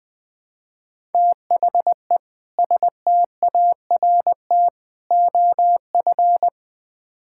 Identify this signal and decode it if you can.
Morse code, heard as a keyed tone, about 20 wpm, 700 Hz. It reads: THE START OF
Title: Build A Phrase - 4-Word Phrases - 20wpm